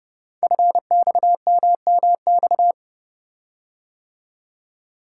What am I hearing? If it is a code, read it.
Morse code, heard as a keyed tone, about 30 wpm, 700 Hz. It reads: FXMMX
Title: Code Groups - 5 Letters - 30wpm